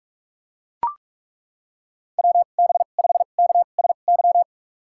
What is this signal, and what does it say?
Morse code, heard as a keyed tone, about 41 wpm, spaced as for 31 wpm, 700 Hz. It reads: WB5KSY
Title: USA Call Signs - 6 Characters - 2x Character Spacing - 45wpm